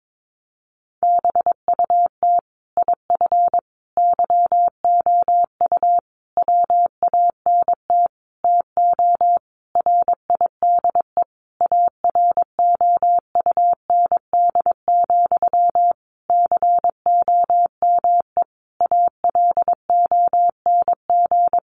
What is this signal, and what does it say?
Morse code, heard as a keyed tone, about 22 wpm, 700 Hz. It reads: BUT IF YOU WANT TO RIDE AROUND, COME ALONG
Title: Sentences from Top 1200 Words 22wpm